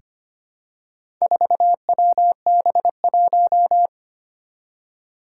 Morse code, heard as a keyed tone, about 25 wpm, 700 Hz. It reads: 4WB1
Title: Four Challenging Letter-Number Combinations 25wpm